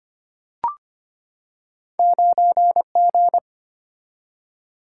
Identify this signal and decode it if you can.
Morse code, heard as a keyed tone, about 25 wpm, 700 Hz. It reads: 9G
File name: Two Letter-Number Combinations - 25wpm